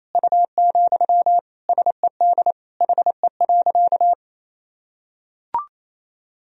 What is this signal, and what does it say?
Morse code, heard as a keyed tone, about 27 wpm, 700 Hz. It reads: U, SED HE.
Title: Sentences from Top 400 Words and Abbreviations 28wpm